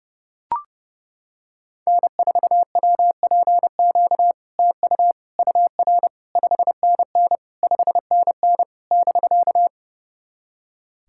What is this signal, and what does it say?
Morse code, heard as a keyed tone, about 32 wpm, 690 Hz. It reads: N4WPQ TU UR 5NN 5NN <BK>
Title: SOTA QSO Practice 30wpm